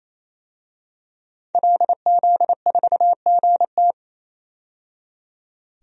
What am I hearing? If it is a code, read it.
Morse code, heard as a keyed tone, about 28 wpm, 700 Hz. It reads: LZ4GT
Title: DXCC-100 Call Signs - 5 Characters - 28wpm